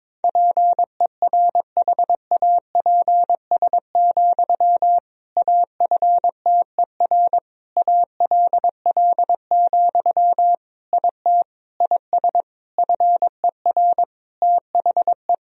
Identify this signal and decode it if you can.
Morse code, heard as a keyed tone, about 22 wpm, 700 Hz. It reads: PERHAPS, AFTER ALL, IT IS FER THE
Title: Sentences from Top 500 Words and Abbreviations 22wpm